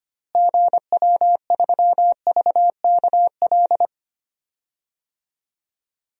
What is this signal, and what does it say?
Morse code, heard as a keyed tone, about 25 wpm, 700 Hz. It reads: GW3VKL
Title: DXCC-100 Call Signs - 6 Characters - Repeated 3x - 25wpm